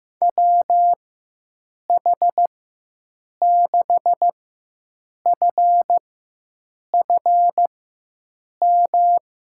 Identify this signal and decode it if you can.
Morse code, heard as a keyed tone, about 16 wpm, spaced as for 7 wpm, 700 Hz. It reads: WH6FFM
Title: USA Call Signs - 6 Characters - 4x Character Spacing - 15wpm